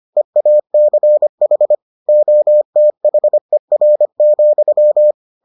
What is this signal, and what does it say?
Morse code, heard as a keyed tone, about 25 wpm, 600 Hz. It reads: EACH OTHER,